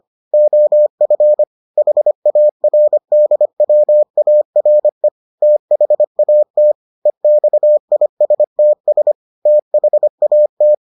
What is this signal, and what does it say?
Morse code, heard as a keyed tone, about 25 wpm, 600 Hz. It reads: OF HARDWARE THAT EXISTS THAT